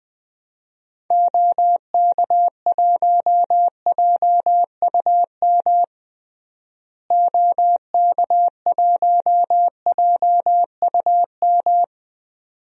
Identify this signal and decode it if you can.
Morse code, heard as a keyed tone, about 21 wpm, 700 Hz. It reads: OK1JUM OK1JUM
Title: DXCC-100 Call Signs - 6 Characters - Repeated 2x - 20wpm